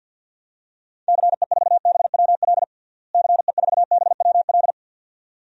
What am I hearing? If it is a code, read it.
Morse code, heard as a keyed tone, about 47 wpm, 700 Hz. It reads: CE4BWL CE4BWL
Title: DXCC-100 Call Signs - 6 Characters - Repeated 2x - 50wpm